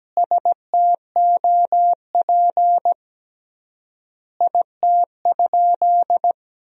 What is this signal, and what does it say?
Morse code, heard as a keyed tone, about 17 wpm, 700 Hz. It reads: STOP IT?
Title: 3x Word-Spacing - Sentences from Top 300 Words - 17wpm